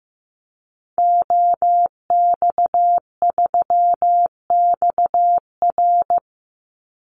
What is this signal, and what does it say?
Morse code, heard as a keyed tone, about 15 wpm, 700 Hz. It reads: OX3XR
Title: Call Signs Repeated 3x - 15wpm